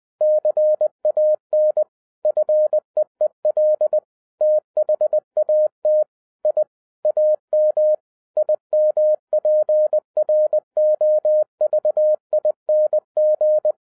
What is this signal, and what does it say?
Morse code, heard as a keyed tone, about 20 wpm, 610 Hz. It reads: CAN FEEL THAT I AM IMPROVING
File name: 2012-01-mp3 / n05-CAME BACK TO HAM RADIO AFTER 30 YEARS OF INTERVAL-20wpm